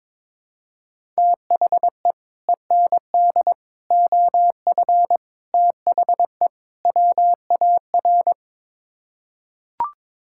Morse code, heard as a keyed tone, about 22 wpm, 700 Hz. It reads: THE END OF THE WAR
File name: practice / 22wpm / Build A Phrase - 5-Word Phrases - 22wpm